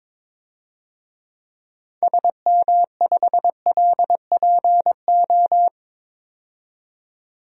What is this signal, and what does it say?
Morse code, heard as a keyed tone, about 21 wpm, 700 Hz. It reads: SM5LPO
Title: DXCC-100 Call Signs - 6 Characters - 22wpm